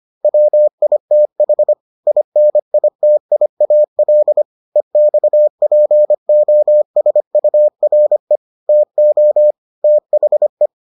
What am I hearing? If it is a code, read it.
Morse code, heard as a keyed tone, about 25 wpm, 600 Hz. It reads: WITH INITIAL EXPOSURE TO THE